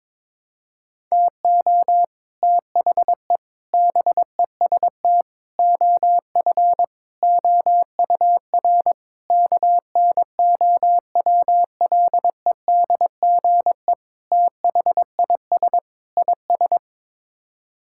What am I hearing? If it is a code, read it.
Morse code, heard as a keyed tone, about 22 wpm, 700 Hz. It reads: TO THE BEST OF OUR KNOWLEDGE THIS IS
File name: practice / 22wpm / Sets of 8 Words - 22wpm